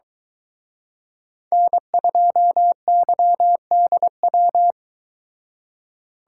Morse code, heard as a keyed tone, about 22 wpm, 700 Hz. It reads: N2YDW